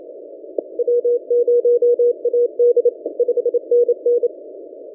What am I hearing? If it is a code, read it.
Morse code, heard as a keyed tone, about 27 wpm, 480 Hz. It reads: W0AD 5NN
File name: posts / v63mj-17cw- -0006z